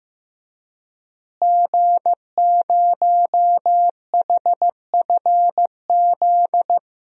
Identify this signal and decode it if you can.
Morse code, heard as a keyed tone, about 16 wpm, 700 Hz. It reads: G0HFZ
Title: DXCC-100 Call Signs - 5 Characters - Repeated 3x - 15wpm